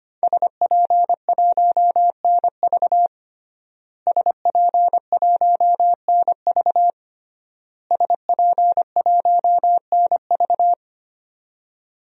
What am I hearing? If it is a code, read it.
Morse code, heard as a keyed tone, about 25 wpm, 700 Hz. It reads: SP1NV SP1NV SP1NV
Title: DXCC-100 Call Signs - 5 Characters - Repeated 3x - 25wpm